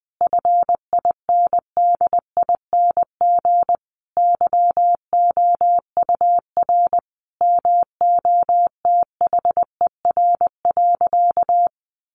Morse code, heard as a keyed tone, about 20 wpm, 700 Hz. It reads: FINDING YOUR MOTHER.
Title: Sentences from Top 1300 Words 20wpm